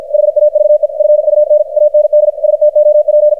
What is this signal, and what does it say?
Morse code, heard as a keyed tone, about 52 wpm, 600 Hz, with radio noise. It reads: HAVE FUN AND STOP
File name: archived / CRSnet